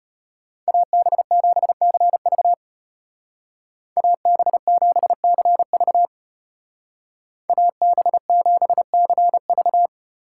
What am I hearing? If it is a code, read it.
Morse code, heard as a keyed tone, about 33 wpm, 700 Hz. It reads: AB7CV AB7CV AB7CV